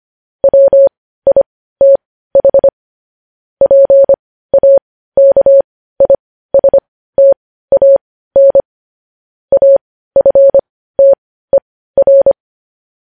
Morse code, heard as a keyed tone, about 25 wpm, spaced as for 15 wpm, 550 Hz. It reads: WITH PAKISTAN AFTER
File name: podcast / News headlines-1x-25x15